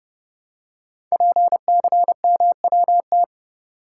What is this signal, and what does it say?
Morse code, heard as a keyed tone, about 30 wpm, 700 Hz. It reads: PCMWT
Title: Code Groups - 5 Letters - 30wpm